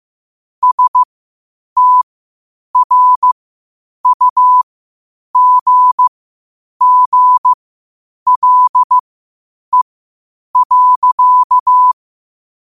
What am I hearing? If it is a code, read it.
Morse code, heard as a keyed tone, about 15 wpm, spaced as for 8 wpm, 1000 Hz. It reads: STRUGGLE.
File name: qotdfiles / 8wpm